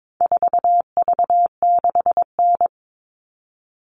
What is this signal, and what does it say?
Morse code, heard as a keyed tone, about 22 wpm, 700 Hz. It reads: 4V6N
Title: All Cut Numbers - 4-Digit 22wpm